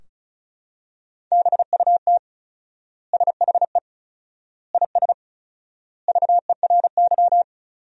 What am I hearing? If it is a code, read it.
Morse code, heard as a keyed tone, about 36 wpm, 700 Hz. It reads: BUT SHE IS VERY
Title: 4x Word-Spacing - Sentences from Top 400 Words - 35wpm